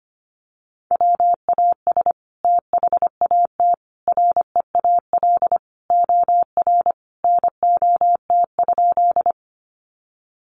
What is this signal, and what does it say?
Morse code, heard as a keyed tone, about 25 wpm, 700 Hz. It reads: WAS THAT REAL OR NOT?
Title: Sentences from Top 600 Words 25wpm